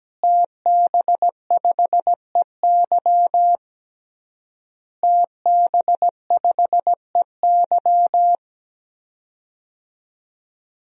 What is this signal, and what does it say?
Morse code, heard as a keyed tone, about 17 wpm, 700 Hz. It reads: TB5EY TB5EY
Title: DXCC-100 Call Signs - 5 Characters - Repeated 3x - 17wpm